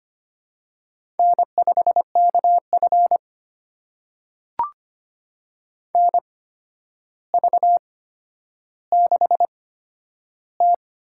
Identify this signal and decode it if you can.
Morse code, heard as a keyed tone, about 25 wpm, spaced as for 11 wpm, 700 Hz. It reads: N5KF NV6T
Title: USA Call Signs - 4 Characters - 8x Character Spacing - 25wpm